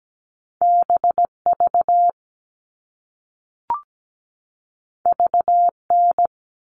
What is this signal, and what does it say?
Morse code, heard as a keyed tone, about 17 wpm, 700 Hz. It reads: BV VN
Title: All Cut Numbers - 2-Digit 17wpm